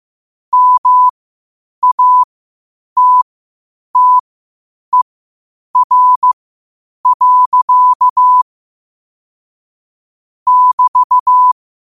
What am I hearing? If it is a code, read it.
Morse code, heard as a keyed tone, about 15 wpm, spaced as for 8 wpm, 1000 Hz. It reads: MATTER. <BT>
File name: qotdfiles / 8wpm